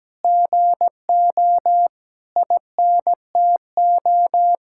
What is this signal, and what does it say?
Morse code, heard as a keyed tone, about 17 wpm, 700 Hz. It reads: GO INTO
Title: Sentences from Top 1000 Words - Review Part 6 - 17wpm